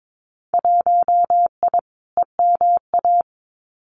Morse code, heard as a keyed tone, about 22 wpm, 700 Hz. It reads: 1I EMA
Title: Winter Field Day Exchanges Encore 22wpm